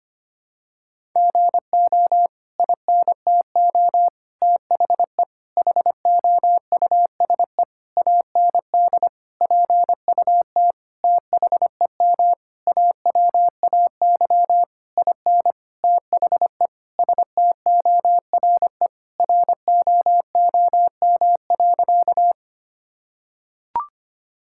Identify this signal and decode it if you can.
Morse code, heard as a keyed tone, about 25 wpm, 700 Hz. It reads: GO INTO THE HOUSE AND PUT THEM AWAY IN THE STORE ROOM.
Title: Sentences from Top 1000 Words - Review Part 5 - 25wpm